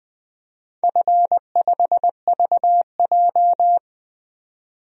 Morse code, heard as a keyed tone, about 20 wpm, 700 Hz. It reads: F5VJ
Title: DXCC-100 Call Signs - 4 Characters - 20wpm